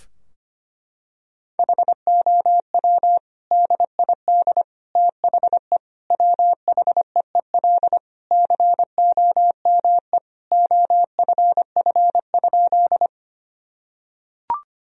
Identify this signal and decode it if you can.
Morse code, heard as a keyed tone, about 25 wpm, 700 Hz. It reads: HOW DID THE WHEEL COME OFF?
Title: Sentences from Top 1000 Words - Review Part 3 - 25wpm